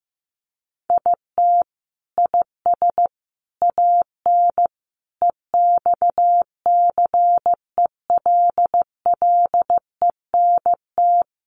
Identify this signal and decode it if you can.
Morse code, heard as a keyed tone, about 15 wpm, 700 Hz. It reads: IT IS AN EXCELLENT
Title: Sentences from Top 1000 Words 15wpm